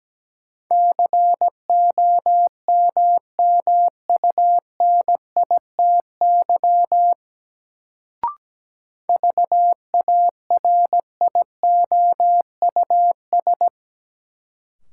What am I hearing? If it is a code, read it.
Morse code, heard as a keyed tone, about 17 wpm, 700 Hz. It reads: COMMUNITY VARIOUS
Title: Sentences from Top 1000 Words - Review Part 4 - New Word Drill - 17wpm